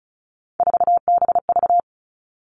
Morse code, heard as a keyed tone, about 36 wpm, 700 Hz. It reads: 4BV
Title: Three Challenging Letter-Number Combinations 35wpm